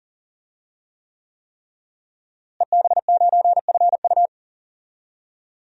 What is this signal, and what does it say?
Morse code, heard as a keyed tone, about 39 wpm, 700 Hz. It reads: EB9FU
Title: DXCC-100 Call Signs - 5 Characters - Repeated 3x - 40wpm